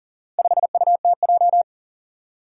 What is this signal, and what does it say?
Morse code, heard as a keyed tone, about 40 wpm, 700 Hz. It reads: 5UTJ
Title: Four Challenging Letter-Number Combinations 40wpm